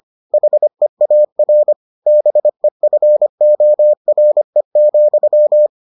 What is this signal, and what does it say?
Morse code, heard as a keyed tone, about 25 wpm, 600 Hz. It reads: HEAR BEFORE,